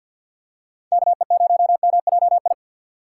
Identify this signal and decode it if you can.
Morse code, heard as a keyed tone, about 46 wpm, 700 Hz. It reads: KE0MJI
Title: USA Call Signs - 6 Characters - 6x Character Spacing - 50wpm